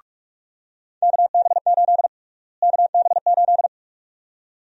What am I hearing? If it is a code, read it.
Morse code, heard as a keyed tone, about 46 wpm, 700 Hz. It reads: KB8 KB8
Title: USA Call Sign Prefixes - Repeated 2x - 45wpm